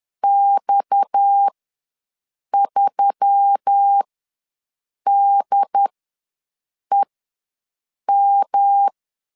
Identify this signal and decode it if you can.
Morse code, heard as a keyed tone, about 10 wpm, spaced as for 6 wpm, 800 Hz. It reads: X3DEM
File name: LEZIONI / 7 GF9Z / 7 G ws 10wpm 3843